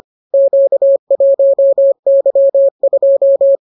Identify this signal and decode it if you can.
Morse code, heard as a keyed tone, about 25 wpm, 550 Hz. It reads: Q1Y2